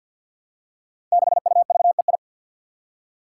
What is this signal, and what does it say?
Morse code, heard as a keyed tone, about 46 wpm, 700 Hz. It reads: 6UVEI
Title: Five Challenging Letter-Number Combinations 50wpm